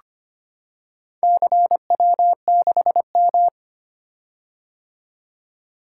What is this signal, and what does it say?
Morse code, heard as a keyed tone, about 25 wpm, 700 Hz. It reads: CW6M
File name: practice / 25wpm / Four Letter-Number Combinations - 25wpm